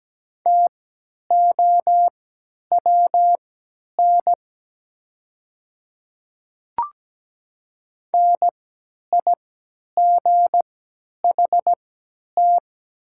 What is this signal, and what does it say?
Morse code, heard as a keyed tone, about 17 wpm, spaced as for 10 wpm, 700 Hz. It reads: TOWN NIGHT
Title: 3x Character Spacing - Top 500 Words - 17wpm